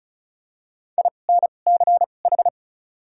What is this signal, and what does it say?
Morse code, heard as a keyed tone, about 34 wpm, spaced as for 25 wpm, 700 Hz. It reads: INCH
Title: 2x Character Spacing - Top 500 Words - 35wpm